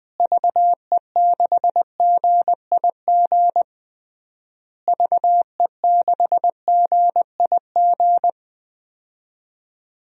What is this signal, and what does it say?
Morse code, heard as a keyed tone, about 21 wpm, 700 Hz. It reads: VE6GIG VE6GIG
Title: DXCC-100 Call Signs - 6 Characters - Repeated 3x - 20wpm